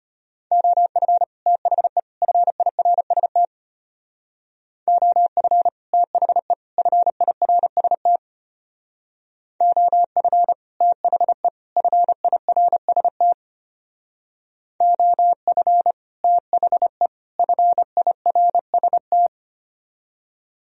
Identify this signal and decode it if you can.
Morse code, heard as a keyed tone, about 31 wpm, 700 Hz. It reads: OF THE FIRST OF THE FIRST OF THE FIRST OF THE FIRST